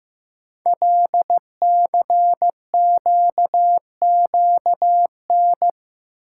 Morse code, heard as a keyed tone, about 15 wpm, 700 Hz. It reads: LCQQN
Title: Code Groups - 5 Letters - 15wpm